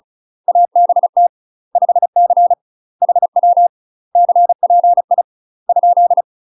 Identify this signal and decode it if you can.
Morse code, heard as a keyed tone, about 34 wpm, 700 Hz. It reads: ABT 5C HW CPI ?
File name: UK MASTER QSOs TRAINING/35WPM